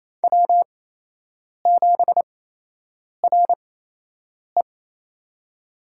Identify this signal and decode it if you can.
Morse code, heard as a keyed tone, about 28 wpm, spaced as for 8 wpm, 700 Hz. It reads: W7RE
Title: USA Call Signs - 4 Characters - 8x Character Spacing - 28wpm